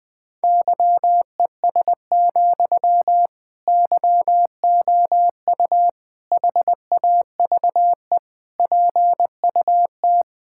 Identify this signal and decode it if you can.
Morse code, heard as a keyed tone, about 20 wpm, 700 Hz. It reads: YES, YOU HAVE PUT
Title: Sentences from Top 1600 Words 20wpm